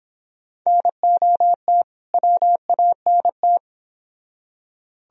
Morse code, heard as a keyed tone, about 27 wpm, 700 Hz. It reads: NOT WANT
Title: Speed-Racing - Sets of 2 Words 22wpm